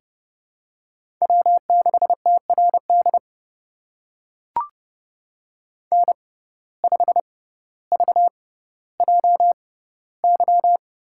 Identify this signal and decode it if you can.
Morse code, heard as a keyed tone, about 30 wpm, spaced as for 15 wpm, 700 Hz. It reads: W6TRD N5VJY
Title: USA Call Signs - 5 Characters - 6x Character Spacing - 30wpm